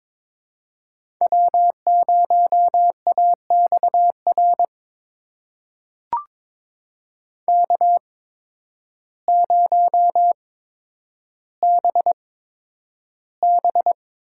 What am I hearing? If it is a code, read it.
Morse code, heard as a keyed tone, about 22 wpm, spaced as for 10 wpm, 700 Hz. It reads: W0AXR K0BB
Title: USA Call Signs - 5 Characters - 8x Character Spacing - 22wpm